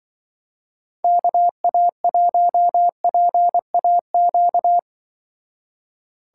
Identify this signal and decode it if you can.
Morse code, heard as a keyed tone, about 25 wpm, 700 Hz. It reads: KA1PAQ
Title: Speed Racing - USA 2x3 Call Signs 20wpm